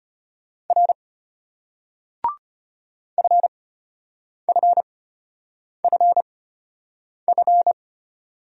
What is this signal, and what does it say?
Morse code, heard as a keyed tone, about 33 wpm, spaced as for 8 wpm, 700 Hz. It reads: R FFFF